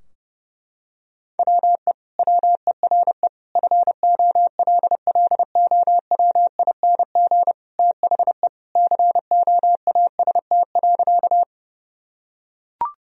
Morse code, heard as a keyed tone, about 32 wpm, 700 Hz. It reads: WE WERE FOLLOWING THE COAST.
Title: Sentences from Top 1000 Words - Review Part 4 - 30wpm